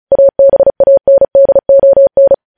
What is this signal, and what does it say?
Morse code, heard as a keyed tone, about 36 wpm, 550 Hz. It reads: ABANDON
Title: News Headlines at 35